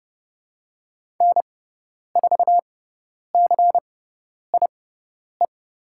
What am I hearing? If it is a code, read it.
Morse code, heard as a keyed tone, about 30 wpm, spaced as for 10 wpm, 700 Hz. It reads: N4CIE
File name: Lesson 18g - Letter m - Call Signs 30wpm